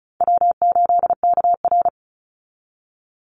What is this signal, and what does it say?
Morse code, heard as a keyed tone, about 34 wpm, 700 Hz. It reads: W8KR
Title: Easy Call Signs Repeated 3x - 35wpm